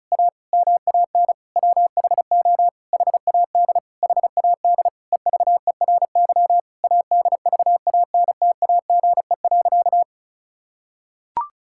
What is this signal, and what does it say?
Morse code, heard as a keyed tone, about 36 wpm, 700 Hz. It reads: A MAN WHO HAD HAD EVERY ADVANTAGE.
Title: Sentences from Top 1000 Words - Review Part 6 - 35wpm